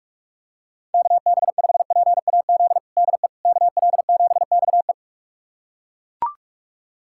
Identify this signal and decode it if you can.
Morse code, heard as a keyed tone, about 41 wpm, 700 Hz. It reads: KB5PAZ DE KL7XE
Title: USA Callsign DE Callsign 45wpm